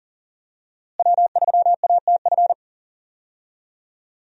Morse code, heard as a keyed tone, about 37 wpm, 700 Hz. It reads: W3ATF
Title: USA Call Signs - 5 Characters - Repeated 2x - 40wpm